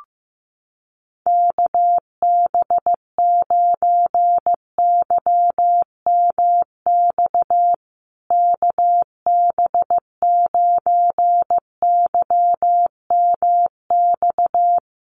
Morse code, heard as a keyed tone, about 14 wpm, 700 Hz. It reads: KB9YMX KB9YMX
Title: USA Call Signs Repeated 3x 15wpm